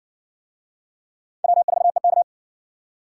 Morse code, heard as a keyed tone, about 60 wpm, 700 Hz. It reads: W4EK